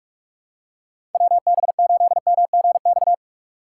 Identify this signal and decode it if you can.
Morse code, heard as a keyed tone, about 48 wpm, 700 Hz. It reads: WB8MGX
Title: USA Call Signs - 6 Characters - 3x Character Spacing - 45wpm